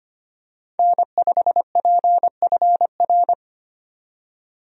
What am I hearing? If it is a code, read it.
Morse code, heard as a keyed tone, about 25 wpm, 700 Hz. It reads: N5PFR
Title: USA Call Signs - 5 Characters - 25wpm